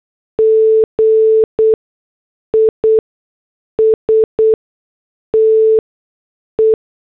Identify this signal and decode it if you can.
Morse code, heard as a keyed tone, about 8 wpm, 440 Hz. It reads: G I S T E
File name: morse / headlines